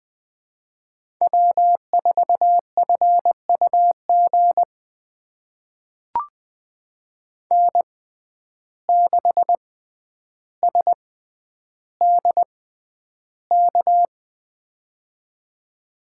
Morse code, heard as a keyed tone, about 20 wpm, spaced as for 10 wpm, 700 Hz. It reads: W4FUG N6SDK
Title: USA Call Signs - 5 Characters - 6x Character Spacing - 20wpm